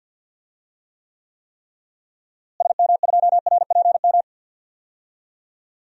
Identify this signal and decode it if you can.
Morse code, heard as a keyed tone, about 48 wpm, 700 Hz. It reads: SM2RPM
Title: DXCC-100 Call Signs - 6 Characters - Repeated 3x - 50wpm